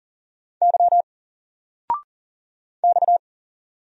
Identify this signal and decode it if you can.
Morse code, heard as a keyed tone, about 40 wpm, 700 Hz. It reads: Y X
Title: Single Character (Letter:Number:Punctuation:Prosign) 40wpm